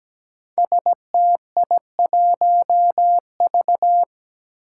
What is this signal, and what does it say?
Morse code, heard as a keyed tone, about 17 wpm, 700 Hz. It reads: STI1V
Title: Five Challenging Letter-Number Combinations 17wpm